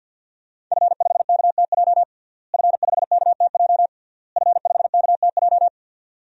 Morse code, heard as a keyed tone, about 50 wpm, 700 Hz. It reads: F5KTJ F5KTJ F5KTJ
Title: DXCC-100 Call Signs - 5 Characters - Repeated 3x - 50wpm